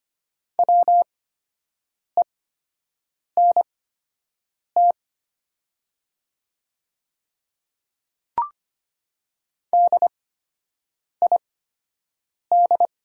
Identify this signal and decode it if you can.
Morse code, heard as a keyed tone, about 25 wpm, spaced as for 7 wpm, 700 Hz. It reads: WENT DID
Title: 8x Character Spacing - Top 300 Words - 25wpm